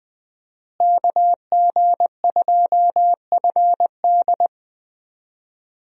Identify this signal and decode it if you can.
Morse code, heard as a keyed tone, about 20 wpm, 700 Hz. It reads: KG2FD
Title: USA Call Signs - 5 Characters - Repeated 2x - 20wpm